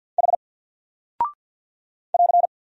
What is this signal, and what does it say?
Morse code, heard as a keyed tone, about 48 wpm, 700 Hz. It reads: H <AR>
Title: Single Character (Letter:Number:Punctuation:Prosign) 50wpm